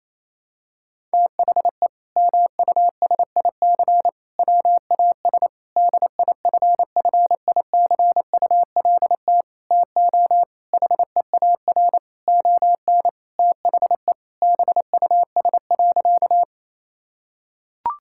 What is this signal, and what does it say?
Morse code, heard as a keyed tone, about 28 wpm, 700 Hz. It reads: THE MUSIC WAS DIFFICULT TO HEAR ON THE BUS.
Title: Sentences from Top 1000 Words - Review Part 6 - 28wpm